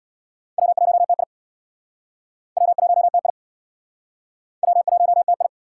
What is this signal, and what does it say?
Morse code, heard as a keyed tone, about 62 wpm, 700 Hz. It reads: W1TI W1TI W1TI
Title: Speed Racing - USA Call Signs 50wpm